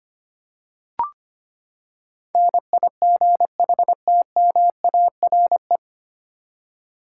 Morse code, heard as a keyed tone, about 25 wpm, 700 Hz. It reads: NIGHTMARE
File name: Nine Letter Words - 25wpm